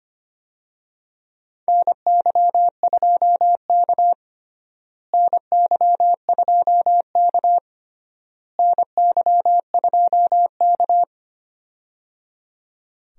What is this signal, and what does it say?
Morse code, heard as a keyed tone, about 25 wpm, 700 Hz. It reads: NY2K NY2K NY2K
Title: USA Call Signs - 4 Characters - Repeated 3x - 25wpm